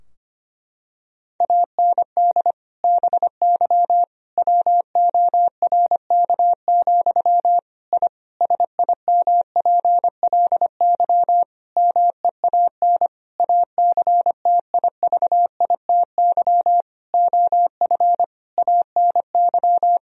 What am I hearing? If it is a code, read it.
Morse code, heard as a keyed tone, about 25 wpm, 700 Hz. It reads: AND BY WORK, I SIMPLY MEAN ACTIVITY OF ANY
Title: Sentences from Top 1000 Words - Review Part 4 - 25wpm